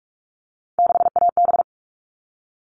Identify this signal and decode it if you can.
Morse code, heard as a keyed tone, about 44 wpm, 700 Hz. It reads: 6AB
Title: All Cut Numbers - 3-Digit 45wpm